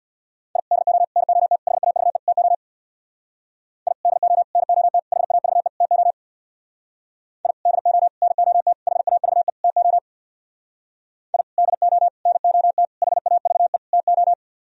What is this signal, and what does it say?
Morse code, heard as a keyed tone, about 63 wpm, 700 Hz. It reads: I DO NOT HAVE TO I DO NOT HAVE TO I DO NOT HAVE TO I DO NOT HAVE TO